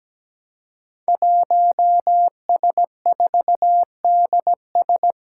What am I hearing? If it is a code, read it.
Morse code, heard as a keyed tone, about 17 wpm, 700 Hz. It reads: 1S4DS
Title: Five Challenging Letter-Number Combinations 17wpm